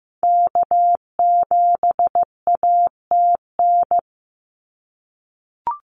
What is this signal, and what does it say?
Morse code, heard as a keyed tone, about 15 wpm, 700 Hz. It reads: K7ATN
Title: NA SOTA Activator Callsigns 15wpm